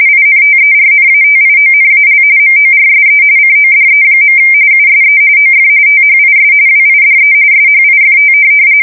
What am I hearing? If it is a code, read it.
RTTY, about 45.45 baud. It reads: INTERNATIONAL MORSE CODE. USE OF THIS WIDE SHIFT
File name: fsk170 / teletype-RTTY